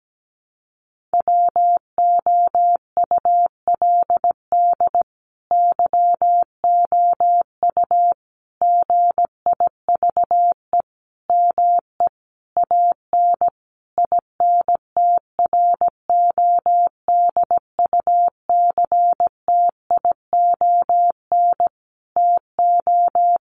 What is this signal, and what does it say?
Morse code, heard as a keyed tone, about 17 wpm, 700 Hz. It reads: WOULD YOU GIVE ME AN INTRODUCTION TO
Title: Sentences from Top 1000 Words 17wpm